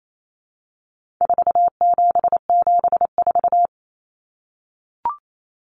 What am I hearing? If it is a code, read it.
Morse code, heard as a keyed tone, about 28 wpm, 700 Hz. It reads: 4774
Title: Common Cut Numbers - 4-Digit 28wpm